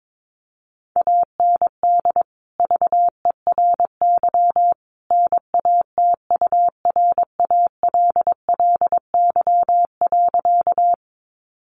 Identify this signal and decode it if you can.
Morse code, heard as a keyed tone, about 22 wpm, 700 Hz. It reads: AND VERY NATURALLY.
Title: Sentences from Top 1300 Words 22wpm